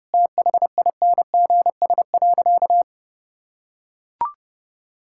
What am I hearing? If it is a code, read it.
Morse code, heard as a keyed tone, about 32 wpm, 700 Hz. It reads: THINGS.
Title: Sentences from Top 1000 Words - Review Part 6 - 30wpm